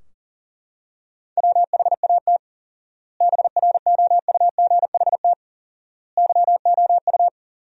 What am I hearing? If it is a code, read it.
Morse code, heard as a keyed tone, about 37 wpm, 700 Hz. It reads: WHAT BROUGHT YOU
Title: 4x Word-Spacing - Sentences from Top 500 Words - 40wpm